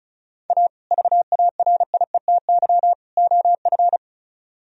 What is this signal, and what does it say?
Morse code, heard as a keyed tone, about 45 wpm, 700 Hz. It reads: A VARIETY OF